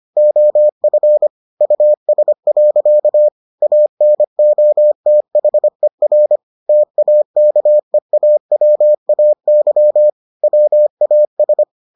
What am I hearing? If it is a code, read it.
Morse code, heard as a keyed tone, about 25 wpm, 600 Hz. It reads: OF US. ANOTHER TAKEAWAY WAS